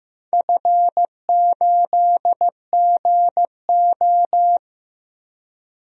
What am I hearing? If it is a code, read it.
Morse code, heard as a keyed tone, about 16 wpm, 700 Hz. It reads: F8GO
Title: DXCC-100 Call Signs - 4 Characters - Repeated 3x - 15wpm